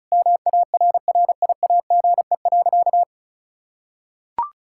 Morse code, heard as a keyed tone, about 36 wpm, 700 Hz. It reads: MARRIAGE.
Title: Sentences from Top 1000 Words - Review Part 5 - 35wpm